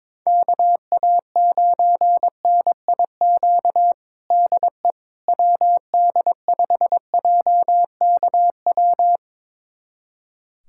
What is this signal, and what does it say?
Morse code, heard as a keyed tone, about 22 wpm, 700 Hz. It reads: KA9NIQ DE WD5JKW
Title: USA Callsign DE Callsign - 6 Characters - 22wpm